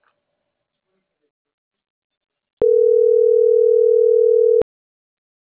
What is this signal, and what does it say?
Ringback tone (North America), 440 + 480 Hz.